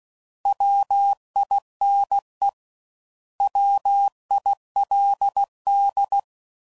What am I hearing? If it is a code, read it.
Morse code, heard as a keyed tone, about 16 wpm, 800 Hz. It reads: WINE WILD
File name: Prefixes / By First Letter / Ws